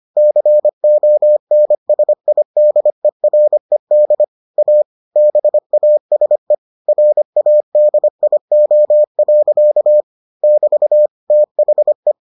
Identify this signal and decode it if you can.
Morse code, heard as a keyed tone, about 25 wpm, 600 Hz. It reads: CONSIDERED A BASE RADIO. <BT> THE